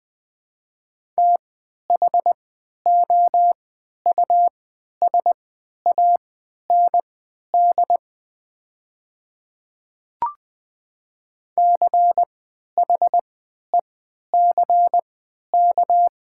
Morse code, heard as a keyed tone, about 20 wpm, spaced as for 11 wpm, 700 Hz. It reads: THOUSAND CHECK
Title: 3x Character Spacing - Top 500 Words - 20wpm